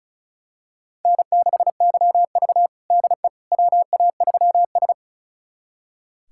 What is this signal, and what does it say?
Morse code, heard as a keyed tone, about 36 wpm, 700 Hz. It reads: N6YV DE WA3S
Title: USA Callsign DE Callsign - 4 Characters - 35wpm